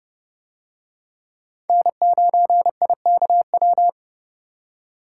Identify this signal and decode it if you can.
Morse code, heard as a keyed tone, about 30 wpm, 700 Hz. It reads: N9IKW